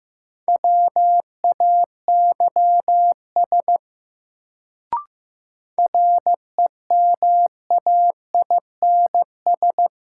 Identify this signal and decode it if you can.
Morse code, heard as a keyed tone, about 15 wpm, 700 Hz. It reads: WAYS REMAINS
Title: Sets of 5 Words Encore - New Word Drill - 15wpm